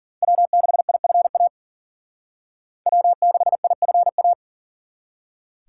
Morse code, heard as a keyed tone, about 50 wpm, 700 Hz. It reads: W6IFA W6IFA
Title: Speed Racing - USA 1x3 Call Signs 40wpm